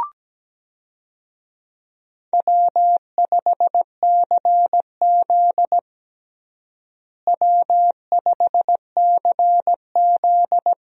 Morse code, heard as a keyed tone, about 17 wpm, 700 Hz. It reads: W5CZ W5CZ
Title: USA Call Signs - 4 Characters - Repeated 2x - 17wpm